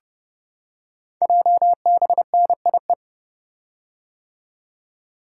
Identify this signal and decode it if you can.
Morse code, heard as a keyed tone, about 29 wpm, 700 Hz. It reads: JBNIE
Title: Code Groups - 5 Letters - 30wpm